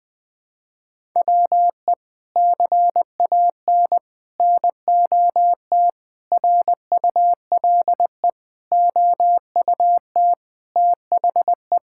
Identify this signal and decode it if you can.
Morse code, heard as a keyed tone, about 21 wpm, 700 Hz. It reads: WE CAN NOT RULE OUT THE
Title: Sets of 8 Words - 20wpm